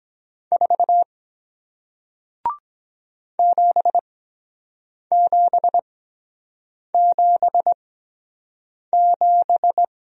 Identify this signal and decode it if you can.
Morse code, heard as a keyed tone, about 23 wpm, spaced as for 7 wpm, 700 Hz. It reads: 4 7777